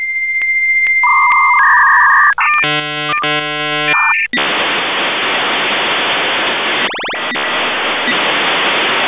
Dial-up modem handshake.
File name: modem / v34bis9